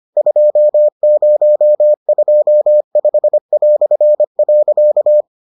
Morse code, heard as a keyed tone, about 25 wpm, 600 Hz. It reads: 2025".